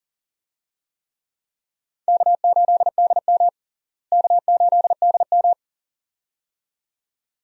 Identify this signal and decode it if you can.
Morse code, heard as a keyed tone, about 37 wpm, 700 Hz. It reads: K8DM K8DM
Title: USA Call Signs - 4 Characters - Repeated 2x - 40wpm